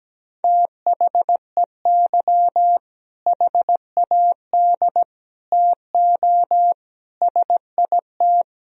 Morse code, heard as a keyed tone, about 17 wpm, 700 Hz. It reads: THEY HAD TO SIT
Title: Sentences from Top 1000 Words - Review Part 1 - 17wpm